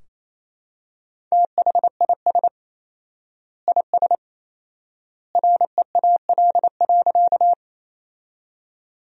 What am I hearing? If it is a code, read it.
Morse code, heard as a keyed tone, about 27 wpm, 700 Hz. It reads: THIS IS REAL.
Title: 4x Word-Spacing - Sentences from Top 300 Words - 28wpm